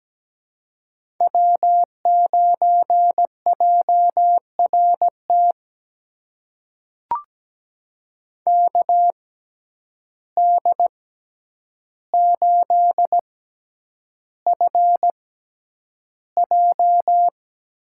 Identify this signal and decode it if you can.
Morse code, heard as a keyed tone, about 17 wpm, spaced as for 9 wpm, 700 Hz. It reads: W9JRT KD8FJ
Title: USA Call Signs - 5 Characters - 6x Character Spacing - 17wpm